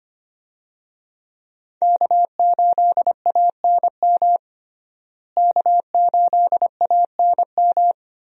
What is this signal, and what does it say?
Morse code, heard as a keyed tone, about 25 wpm, 700 Hz. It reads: K8ANM K8ANM
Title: USA Call Signs - 5 Characters - Repeated 3x - 25wpm